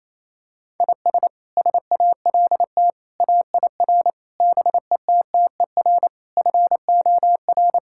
Morse code, heard as a keyed tone, about 28 wpm, 700 Hz. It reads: IS SALT AIR BETTER FOR
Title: Sentences from Top 1000 Words - Review Part 5 - 28wpm